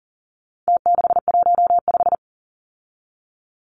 Morse code, heard as a keyed tone, about 40 wpm, 700 Hz. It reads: T615
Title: Numbers - Contest Style - 4-Digit 40wpm